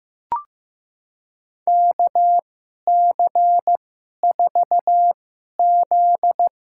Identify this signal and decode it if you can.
Morse code, heard as a keyed tone, about 15 wpm, spaced as for 11 wpm, 700 Hz. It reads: KC4Z
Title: USA Call Signs - 6 Characters - 2x Character Spacing - 15wpm